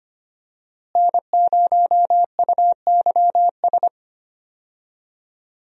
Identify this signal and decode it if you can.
Morse code, heard as a keyed tone, about 25 wpm, 700 Hz. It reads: N0UYS
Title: USA Call Signs - 5 Characters - Repeated 3x - 25wpm